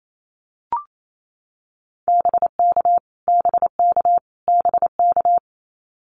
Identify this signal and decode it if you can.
Morse code, heard as a keyed tone, about 28 wpm, 700 Hz. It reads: BK BK BK
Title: Abbreviations Repeated 3x - 28wpm